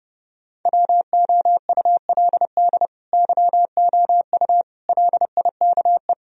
Morse code, heard as a keyed tone, about 33 wpm, 700 Hz. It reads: WOULD YOU LIKE